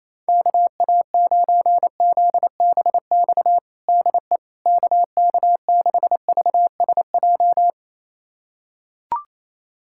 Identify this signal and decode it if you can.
Morse code, heard as a keyed tone, about 28 wpm, 700 Hz. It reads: KA9ZBX DE KK6VSJ
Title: USA Callsign DE Callsign - 6 Characters - 28wpm